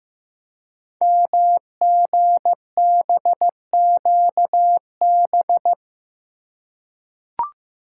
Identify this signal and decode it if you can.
Morse code, heard as a keyed tone, about 15 wpm, 700 Hz. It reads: MGBQB
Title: Code Groups - 5 Letters - 15wpm